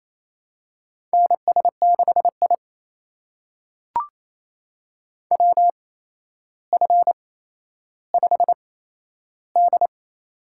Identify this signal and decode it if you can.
Morse code, heard as a keyed tone, about 28 wpm, spaced as for 12 wpm, 700 Hz. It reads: NS6I WF5D
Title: USA Call Signs - 4 Characters - 8x Character Spacing - 28wpm